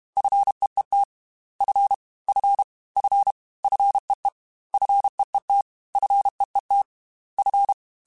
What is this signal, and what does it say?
Morse code, heard as a keyed tone, about 32 wpm, 800 Hz. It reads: FEET F F F FEE FEET FEET F
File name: F and L / F and L mp3 / F and L Prefixes 3x2 Twice @30